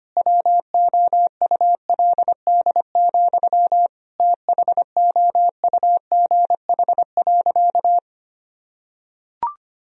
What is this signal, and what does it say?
Morse code, heard as a keyed tone, about 25 wpm, 700 Hz. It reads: WOULD, THOUGH.
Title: Sentences from Top 1000 Words - Review Part 2 - 25wpm